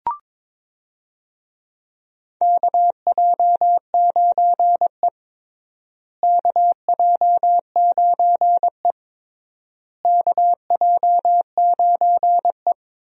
Morse code, heard as a keyed tone, about 22 wpm, 700 Hz. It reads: KJ9E KJ9E KJ9E
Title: USA Call Signs - 4 Characters - Repeated 3x - 22wpm